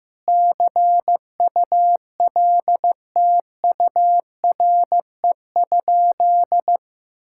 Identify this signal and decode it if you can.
Morse code, heard as a keyed tone, about 15 wpm, 700 Hz. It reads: CULTURE?
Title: Sentences from Top 1000 Words - Review Part 6 - 15wpm